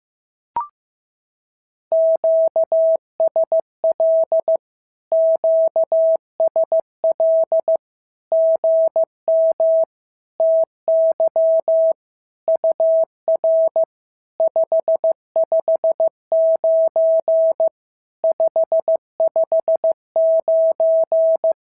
Morse code, heard as a keyed tone, about 15 wpm, 650 Hz. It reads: QSL QSL GM TY UR 559 559
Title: SOTA QSO Practice 15wpm